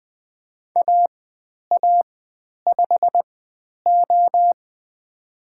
Morse code, heard as a keyed tone, about 20 wpm, spaced as for 10 wpm, 700 Hz. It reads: AA5O
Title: 109 Lesson 18g - Letter m - Call Signs 20wpm